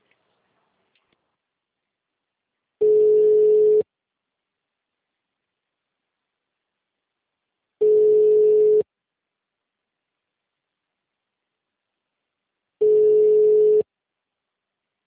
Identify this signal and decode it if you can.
Ringback tone (Europe, 425 Hz).